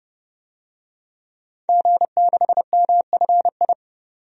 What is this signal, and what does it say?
Morse code, heard as a keyed tone, about 30 wpm, 700 Hz. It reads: G6MFI
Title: DXCC-100 Call Signs - 5 Characters - 30wpm